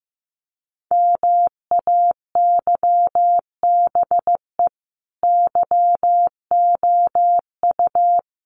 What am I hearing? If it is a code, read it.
Morse code, heard as a keyed tone, about 15 wpm, 700 Hz. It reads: MAYBE YOU
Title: Sentences from Top 1400 Words 15wpm